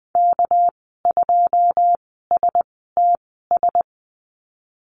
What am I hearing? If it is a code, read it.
Morse code, heard as a keyed tone, about 20 wpm, spaced as for 15 wpm, 700 Hz. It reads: K2STS
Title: 213 Lesson 34g - Number 8 - Call Signs 20wpm